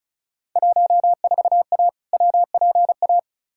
Morse code, heard as a keyed tone, about 36 wpm, 700 Hz. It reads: 14A WPA
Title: Field Day Exchanges Encore - 35wpm